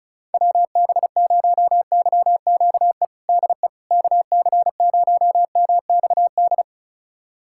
Morse code, heard as a keyed tone, about 36 wpm, 700 Hz. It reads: WB0YQE DE KC0MXD
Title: USA Callsign DE Callsign - 6 Characters - 35wpm